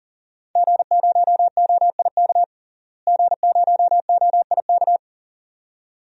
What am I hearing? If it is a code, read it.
Morse code, heard as a keyed tone, about 40 wpm, 700 Hz. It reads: G0OIK G0OIK
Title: DXCC-100 Call Signs - 5 Characters - Repeated 2x - 40wpm